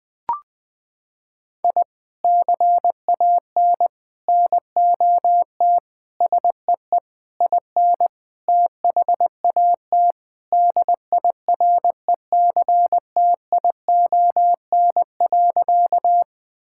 Morse code, heard as a keyed tone, about 21 wpm, 700 Hz. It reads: I CAN NOT SEE IN THAT DIRECTION.
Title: Sentences from Top 1000 Words - Review Part 5 - 20wpm